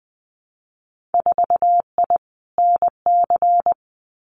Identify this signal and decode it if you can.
Morse code, heard as a keyed tone, about 20 wpm, 700 Hz. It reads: 4I NC
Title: Winter Field Day Exchanges 20wpm